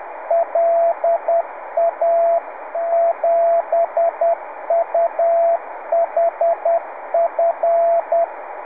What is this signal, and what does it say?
Morse code, heard as a keyed tone, about 10 wpm, 680 Hz, with radio noise. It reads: LA7UHF
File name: on1alj / la7uhf.70cm. .2248